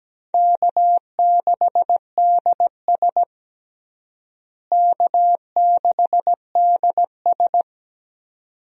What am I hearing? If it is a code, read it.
Morse code, heard as a keyed tone, about 17 wpm, 700 Hz. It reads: K6DS K6DS
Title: USA Call Signs - 4 Characters - Repeated 2x - 17wpm